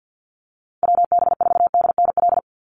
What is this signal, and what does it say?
Morse code, heard as a keyed tone, about 50 wpm, 700 Hz. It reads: WB4DNL
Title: Call Signs - 50wpm